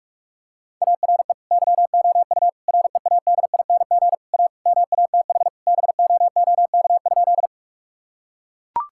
Morse code, heard as a keyed tone, about 41 wpm, 700 Hz. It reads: ARE YOU READING A MATH BOOK?
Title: Sentences from Top 1000 Words - Review Part 6 - 45wpm